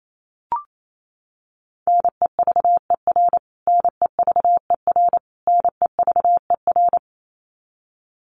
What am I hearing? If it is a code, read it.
Morse code, heard as a keyed tone, about 28 wpm, 700 Hz. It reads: NEVER NEVER NEVER
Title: Top 500 Words Repeated 28wpm